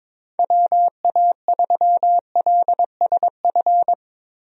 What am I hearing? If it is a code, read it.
Morse code, heard as a keyed tone, about 22 wpm, 700 Hz. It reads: WA3LSF
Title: USA Call Signs - 6 Characters - Repeated 2x - 22wpm